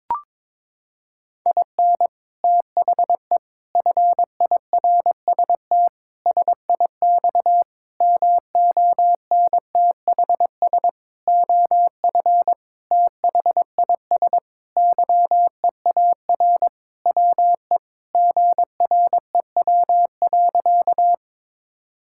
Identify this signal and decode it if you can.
Morse code, heard as a keyed tone, about 21 wpm, 700 Hz. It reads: IN THE FIRST SIX MONTHS OF THIS YEAR WE GREW.
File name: practice / 22wpm / Sentences with 8-Word Phrases - 22wpm